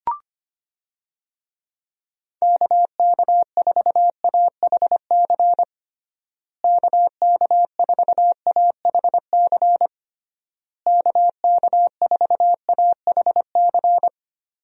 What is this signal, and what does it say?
Morse code, heard as a keyed tone, about 25 wpm, 700 Hz. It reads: KK4AHC KK4AHC KK4AHC
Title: USA Call Signs - 6 Characters - Repeated 3x - 25wpm